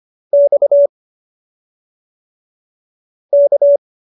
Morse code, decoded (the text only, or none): X K